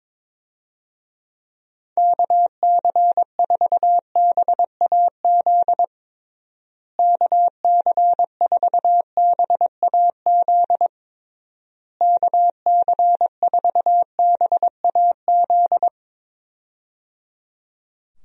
Morse code, heard as a keyed tone, about 22 wpm, 700 Hz. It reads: KC4BAZ KC4BAZ KC4BAZ
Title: USA Call Signs - 6 Characters - Repeated 3x - 22wpm